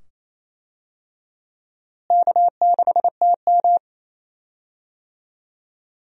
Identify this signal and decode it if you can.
Morse code, heard as a keyed tone, about 28 wpm, 700 Hz. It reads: K6TM
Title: USA Call Signs - 4 Characters - Repeated 2x - 28wpm